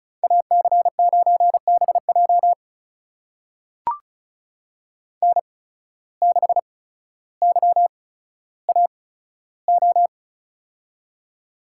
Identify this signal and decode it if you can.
Morse code, heard as a keyed tone, about 34 wpm, spaced as for 15 wpm, 700 Hz. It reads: AC9BJ N6YAO
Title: USA Call Signs - 5 Characters - 8x Character Spacing - 35wpm